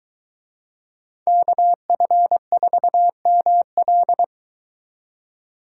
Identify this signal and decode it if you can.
Morse code, heard as a keyed tone, about 22 wpm, 700 Hz. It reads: KF4ML